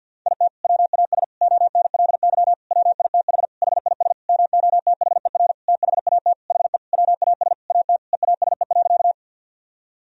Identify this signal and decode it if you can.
Morse code, heard as a keyed tone, about 52 wpm, 700 Hz. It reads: IT WAS ONLY WITH HIS MOTHER THAT HE WAS AT EASE.
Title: Sentences from Top 1000 Words - Review Part 1 - 50wpm